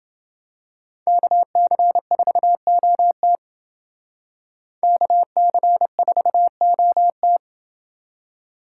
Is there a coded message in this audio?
Yes: Morse code from 1.1 to 7.4 s, heard as a keyed tone, about 30 wpm, 700 Hz. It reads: KC4OT KC4OT